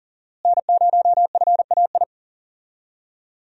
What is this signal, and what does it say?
Morse code, heard as a keyed tone, about 40 wpm, 700 Hz. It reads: N0FAI
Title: USA Call Signs - 5 Characters - Repeated 3x - 40wpm